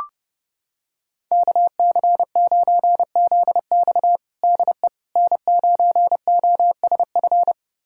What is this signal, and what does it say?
Morse code, heard as a keyed tone, about 32 wpm, 700 Hz. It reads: KC9ZX DE N9OSF
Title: USA Callsign DE Callsign - 5 Characters - 30wpm